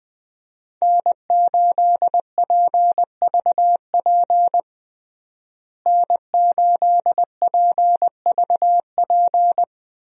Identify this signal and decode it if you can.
Morse code, heard as a keyed tone, about 20 wpm, 700 Hz. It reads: N8PVP N8PVP
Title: USA Call Signs - 5 Characters - Repeated 3x - 20wpm